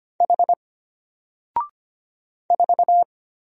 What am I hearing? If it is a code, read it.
Morse code, heard as a keyed tone, about 25 wpm, 700 Hz. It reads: H 4
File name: Single-Letter-Number-25wpm